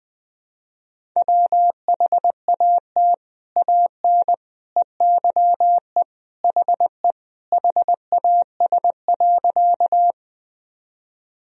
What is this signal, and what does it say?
Morse code, heard as a keyed tone, about 20 wpm, 700 Hz. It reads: WHAT AN EYE HE HAS.
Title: Sentences from Top 1000 Words - Review Part 1 - 20wpm